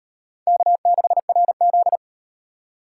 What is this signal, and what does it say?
Morse code, heard as a keyed tone, about 33 wpm, 700 Hz. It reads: K6RZ